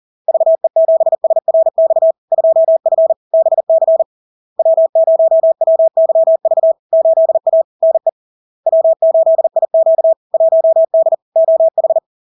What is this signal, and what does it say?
Morse code, heard as a keyed tone, about 40 wpm, 650 Hz. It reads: VE7SRX 2F BC W0WYV 8A NE W8IQ 1D OH
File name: LARK-FD-2022-Log-in-CW-40wpm-650-Hz